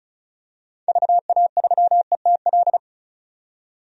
Morse code, heard as a keyed tone, about 36 wpm, 700 Hz. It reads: VA3ETL
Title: DXCC-100 Call Signs - 6 Characters - 35wpm